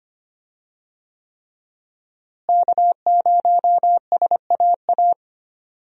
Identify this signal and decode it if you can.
Morse code, heard as a keyed tone, about 25 wpm, 700 Hz. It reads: K0SAA
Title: USA Call Signs - 5 Characters - Repeated 3x - 25wpm